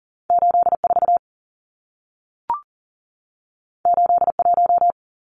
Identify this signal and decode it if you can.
Morse code, heard as a keyed tone, about 40 wpm, 700 Hz. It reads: 84 81
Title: Numbers - 2-Digit 40wpm